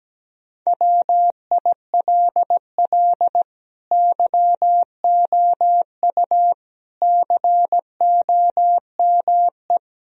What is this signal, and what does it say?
Morse code, heard as a keyed tone, about 17 wpm, 700 Hz. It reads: WILL YOU COME
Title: Sentences from Top 1000 Words - Review Part 4 - 17wpm